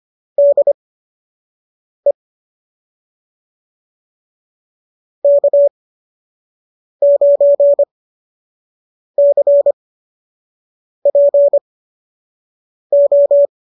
Morse code, heard as a keyed tone, about 25 wpm, spaced as for 6 wpm, 590 Hz. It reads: DE K9CPO